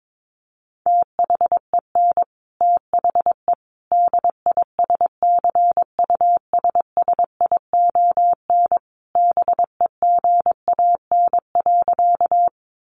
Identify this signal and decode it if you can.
Morse code, heard as a keyed tone, about 22 wpm, 700 Hz. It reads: THEN THE DISCUSSION BEGAN.
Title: Sentences from Top 1100 Words 22wpm